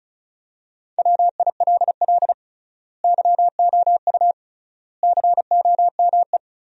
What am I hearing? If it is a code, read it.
Morse code, heard as a keyed tone, about 36 wpm, 700 Hz. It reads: WILL YOU COME
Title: 3x Word-Spacing - Sentences from Top 200 Words - 35wpm